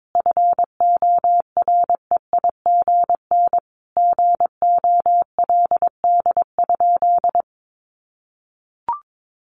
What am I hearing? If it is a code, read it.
Morse code, heard as a keyed tone, about 22 wpm, 700 Hz. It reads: FOREIGN GOLD?
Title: Sentences from Top 1500 Words 22wpm